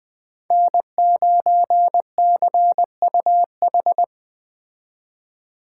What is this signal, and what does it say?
Morse code, heard as a keyed tone, about 20 wpm, 700 Hz. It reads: N9CUH
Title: Speed Racing - USA 1x3 Call Signs 17wpm